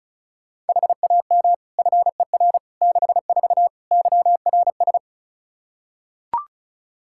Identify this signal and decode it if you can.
Morse code, heard as a keyed tone, about 36 wpm, 700 Hz. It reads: HAM FER 64 YRS
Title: K1USN SST QSO Practice 35wpm